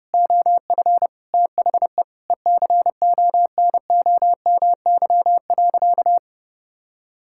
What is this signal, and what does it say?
Morse code, heard as a keyed tone, about 32 wpm, 700 Hz. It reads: OF THE ECONOMY.
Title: Sentences from Top 1000 Words - Review Part 6 - 30wpm